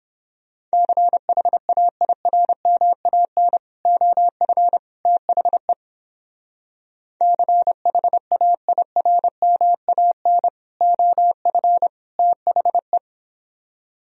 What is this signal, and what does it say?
Morse code, heard as a keyed tone, about 31 wpm, 700 Hz. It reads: CHAIRMAN OF THE CHAIRMAN OF THE